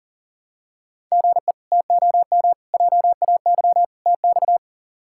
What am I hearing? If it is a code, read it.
Morse code, heard as a keyed tone, about 40 wpm, 700 Hz. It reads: GE TOM JAY TX
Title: K1USN SST QSO Practice - 40wpm